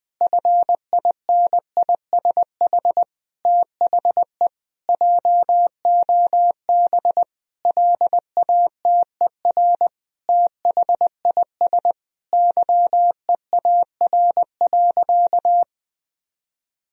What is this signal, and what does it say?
Morse code, heard as a keyed tone, about 20 wpm, 700 Hz. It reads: FINISH THE JOB LATER THIS YEAR.
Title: Sentences from Top 1000 Words - Review Part 3 - 20wpm